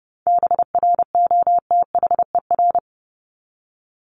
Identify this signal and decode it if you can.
Morse code, heard as a keyed tone, about 30 wpm, 700 Hz. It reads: BROTHER
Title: Top 1751-2000 Words 30wpm